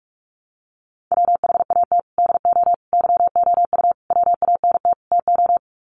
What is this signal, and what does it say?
Morse code, heard as a keyed tone, about 44 wpm, 700 Hz. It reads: WHAT DO YOU WANT TO
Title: Sets of 5 Words 45wpm